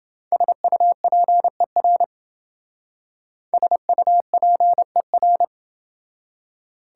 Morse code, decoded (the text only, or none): SUPER SUPER